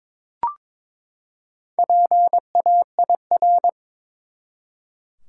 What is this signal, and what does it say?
Morse code, heard as a keyed tone, about 22 wpm, 700 Hz. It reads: PAIR
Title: Four Letter Words - 22wpm